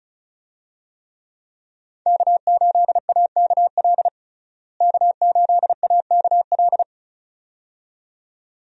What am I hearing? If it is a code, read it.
Morse code, heard as a keyed tone, about 34 wpm, 700 Hz. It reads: K8AKL K8AKL
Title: USA Call Signs - 5 Characters - Repeated 2x - 35wpm